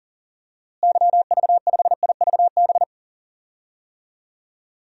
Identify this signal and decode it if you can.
Morse code, heard as a keyed tone, about 40 wpm, 700 Hz. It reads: YV5IVB
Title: DXCC-100 Call Signs - 6 Characters - Repeated 3x - 40wpm